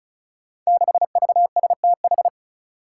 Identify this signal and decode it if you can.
Morse code, heard as a keyed tone, about 36 wpm, 700 Hz. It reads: 6VSTH
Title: Five Challenging Letter-Number Combinations 35wpm